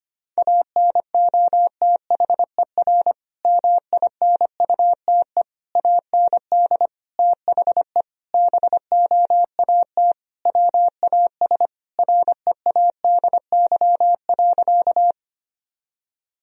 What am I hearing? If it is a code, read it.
Morse code, heard as a keyed tone, about 25 wpm, 700 Hz. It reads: ANOTHER MINUTE AND THE BOAT WAS READY.
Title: Sentences from Top 1000 Words - Review Part 3 - 25wpm